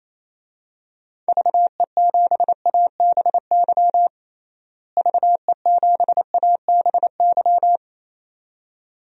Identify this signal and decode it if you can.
Morse code, heard as a keyed tone, about 28 wpm, 700 Hz. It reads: VE7ABY VE7ABY
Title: DXCC-100 Call Signs - 6 Characters - Repeated 2x - 28wpm